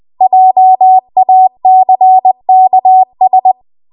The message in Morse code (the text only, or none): JACKS